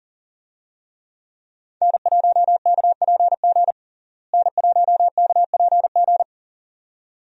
Morse code, heard as a keyed tone, about 40 wpm, 700 Hz. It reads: N1KPG N1KPG
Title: USA Call Signs - 5 Characters - Repeated 2x - 40wpm